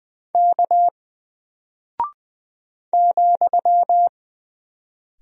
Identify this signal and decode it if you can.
Morse code, heard as a keyed tone, about 20 wpm, 700 Hz. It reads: K ,
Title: Single Character (Letter:Number:Punctuation:Prosign) Alt 20wpm